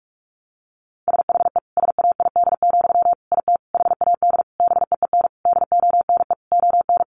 Morse code, heard as a keyed tone, about 44 wpm, 700 Hz. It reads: SHE SAID, IT HAD BEEN DONE ON
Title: Sentences from Top 800 Words 45wpm